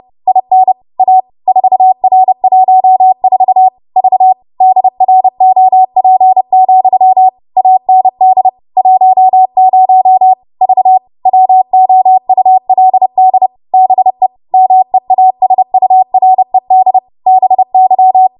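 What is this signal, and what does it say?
Morse code, heard as a keyed tone, about 30 wpm, 750 Hz. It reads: IN A 4R14 V DROP, AND 10 V WOULD BE MEASURED BY
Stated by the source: W1AW Web Code Practice Files, 30 WPM run from ARRL; text from April 2023 QST.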